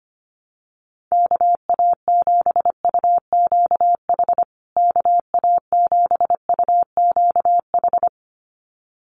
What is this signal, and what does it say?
Morse code, heard as a keyed tone, about 25 wpm, 700 Hz. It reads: KA7UQH KA7UQH
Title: USA Call Signs Repeated 2x 25wpm